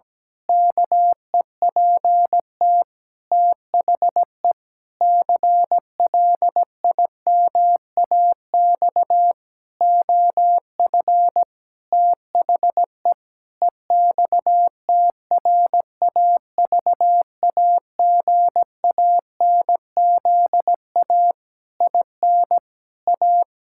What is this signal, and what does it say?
Morse code, heard as a keyed tone, about 17 wpm, 700 Hz. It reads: KEPT THE CLIMAX OF THE EXTRAVAGANZA IN A
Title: Pangram Challenge v2 - 17wpm